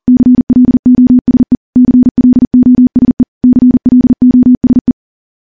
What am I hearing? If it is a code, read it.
Morse code, heard as a keyed tone, about 40 wpm, 250 Hz. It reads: CLOSE CLOSE CLOSE
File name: morse / morse-training-40WPM